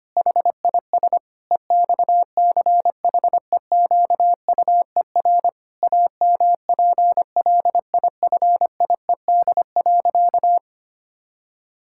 Morse code, heard as a keyed tone, about 25 wpm, 700 Hz. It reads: HIS EXCHEQUER AMPLIFIED.
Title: Pangram Challenge v2 - 25wpm